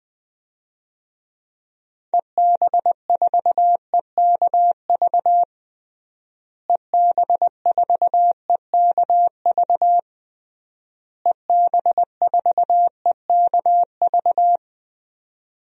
Morse code, heard as a keyed tone, about 20 wpm, 700 Hz. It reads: EB4EKV EB4EKV EB4EKV
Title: DXCC-100 Call Signs - 6 Characters - Repeated 3x - 20wpm